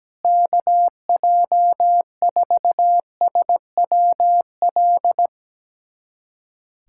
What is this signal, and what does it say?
Morse code, heard as a keyed tone, about 17 wpm, 700 Hz. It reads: KJ4SWL
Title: USA Callsign DE Callsign 17wpm